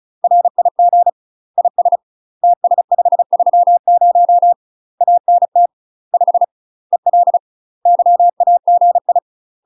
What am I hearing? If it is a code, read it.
Morse code, heard as a keyed tone, about 35 wpm, 700 Hz. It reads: RIG IS TS530 ANT 5 EL YAGI
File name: UK MASTER QSOs TRAINING / 35WPM / UK MASTER QSOs 0030